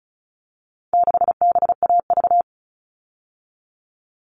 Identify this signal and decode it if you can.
Morse code, heard as a keyed tone, about 34 wpm, 700 Hz. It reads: 6BAV
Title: All Cut Numbers - 4-Digit 35wpm